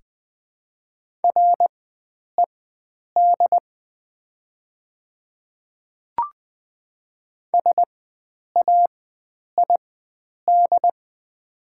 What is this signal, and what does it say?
Morse code, heard as a keyed tone, about 20 wpm, spaced as for 9 wpm, 700 Hz. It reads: RED SAID
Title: 4x Character Spacing - Top 500 Words - 20wpm